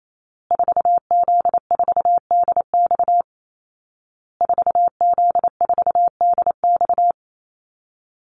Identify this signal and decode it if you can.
Morse code, heard as a keyed tone, about 28 wpm, 700 Hz. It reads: 4Z4DX 4Z4DX
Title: Call Signs Repeated 2x - 28wpm